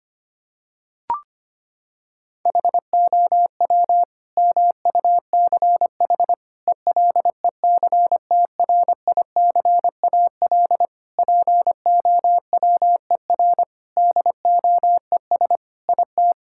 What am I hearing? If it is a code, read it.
Morse code, heard as a keyed tone, about 25 wpm, 700 Hz. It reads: HOW MUCH ELECTRICAL POWER DOES IT
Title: Sentences from Top 1000 Words - Review Part 6 - 25wpm